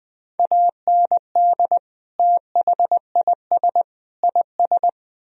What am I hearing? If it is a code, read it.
Morse code, heard as a keyed tone, about 20 wpm, 700 Hz. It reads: AND THIS IS
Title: Speed-Racing - Sets of 3 Words 17wpm